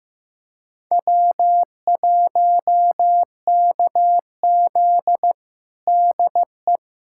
Morse code, heard as a keyed tone, about 15 wpm, 700 Hz. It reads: W1KZ DE
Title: USA Callsign DE Callsign - 4 Characters - 15wpm